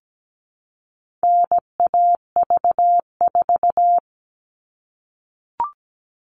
Morse code, heard as a keyed tone, about 17 wpm, 700 Hz. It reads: NAV4
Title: All Cut Numbers - 4-Digit 17wpm